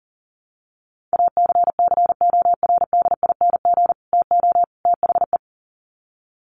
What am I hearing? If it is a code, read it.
Morse code, heard as a keyed tone, about 40 wpm, 700 Hz. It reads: ACCORDING TO THE
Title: Sets of 3 Words 40wpm